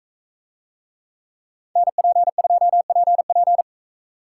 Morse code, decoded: NP2PP

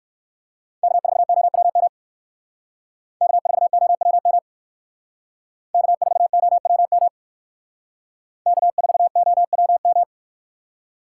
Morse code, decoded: K4OWM K4OWM K4OWM K4OWM

